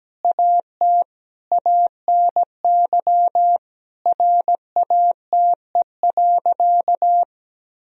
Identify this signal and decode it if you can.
Morse code, heard as a keyed tone, about 17 wpm, 700 Hz. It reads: AT ANY RATE.
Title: Sentences from Top 1000 Words - Review Part 4 - 17wpm